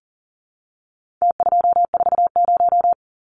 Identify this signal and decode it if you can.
Morse code, heard as a keyed tone, about 40 wpm, 700 Hz. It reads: T240
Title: Numbers - Contest Style - 4-Digit 40wpm